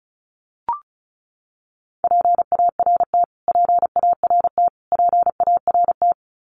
Morse code, heard as a keyed tone, about 34 wpm, 700 Hz. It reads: PART PART PART
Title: Top 500 Words Repeated 35wpm